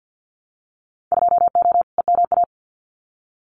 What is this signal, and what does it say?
Morse code, heard as a keyed tone, about 50 wpm, 700 Hz. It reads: 2O EMA
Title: Winter Field Day Exchanges Encore 50wpm